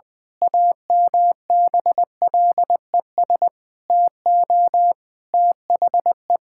Morse code, heard as a keyed tone, about 20 wpm, 700 Hz. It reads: AMBLES TO THE
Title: Pangram Challenge v2 - 20wpm